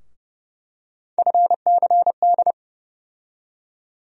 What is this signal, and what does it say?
Morse code, heard as a keyed tone, about 30 wpm, 700 Hz. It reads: FCD
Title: USA Call Sign Postfixes - 3 Letters 30wpm